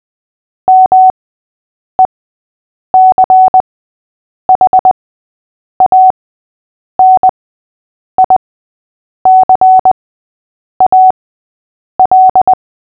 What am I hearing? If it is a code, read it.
Morse code, heard as a keyed tone, about 20 wpm, spaced as for 8 wpm, 750 Hz. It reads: MECHANICAL
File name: cw / fists / Short-Story-6-8wpm